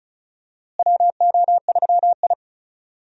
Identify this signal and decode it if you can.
Morse code, heard as a keyed tone, about 36 wpm, 700 Hz. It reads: WO3I
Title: USA Call Signs - 4 Characters - Repeated 3x - 35wpm